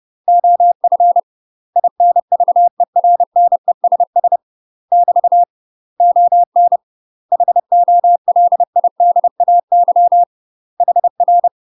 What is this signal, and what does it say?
Morse code, heard as a keyed tone, about 30 wpm, 700 Hz. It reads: OF INVERNESS <BT> ON HOLIDAY HR
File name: UK MASTER QSOs TRAINING / 30WPM / UK MASTER QSOs 0050